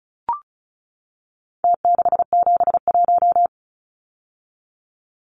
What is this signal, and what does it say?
Morse code, heard as a keyed tone, about 34 wpm, 700 Hz. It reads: T671
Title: Numbers - Contest Style - 4-Digit 35wpm